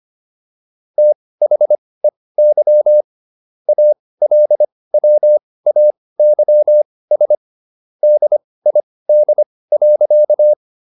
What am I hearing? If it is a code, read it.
Morse code, heard as a keyed tone, about 25 wpm, spaced as for 18 wpm, 600 Hz. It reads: THEY ALWAYS DID.